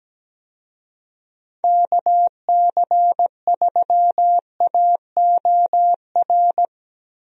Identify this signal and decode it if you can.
Morse code, heard as a keyed tone, about 17 wpm, 700 Hz. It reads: KC3AOR
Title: USA Call Signs - 6 Characters - Repeated 2x - 17wpm